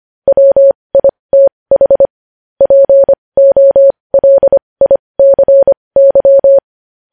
Morse code, heard as a keyed tone, about 25 wpm, spaced as for 20 wpm, 550 Hz. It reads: WITH POLICY
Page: News Headlines in Morse Code at 20 WPM